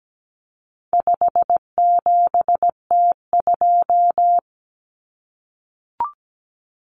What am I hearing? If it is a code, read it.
Morse code, heard as a keyed tone, about 17 wpm, 700 Hz. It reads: 57T2
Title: Common Cut Numbers - 4-Digit 17wpm